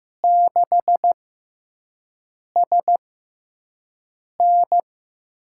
Morse code, heard as a keyed tone, about 15 wpm, spaced as for 5 wpm, 700 Hz. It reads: 6SN
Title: USA Call Signs - 6x Character Spacing - 15wpm